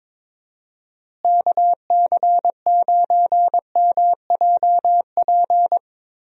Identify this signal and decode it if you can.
Morse code, heard as a keyed tone, about 22 wpm, 700 Hz. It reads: KC9MJP
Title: USA Call Signs - 6 Characters - Repeated 2x - 22wpm